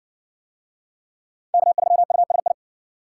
Morse code, heard as a keyed tone, about 56 wpm, 700 Hz. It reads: K3USI